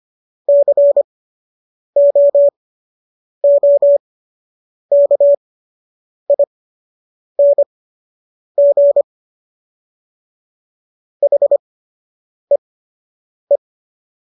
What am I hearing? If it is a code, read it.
Morse code, heard as a keyed tone, about 25 wpm, spaced as for 8 wpm, 570 Hz. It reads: COOKING HEE
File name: qso006 08